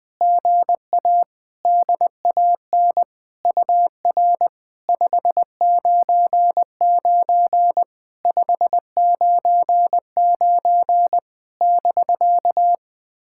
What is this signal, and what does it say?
Morse code, heard as a keyed tone, about 21 wpm, 700 Hz. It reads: GA DAN UR 599 599 <BK>
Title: SOTA QSO Practice 20wpm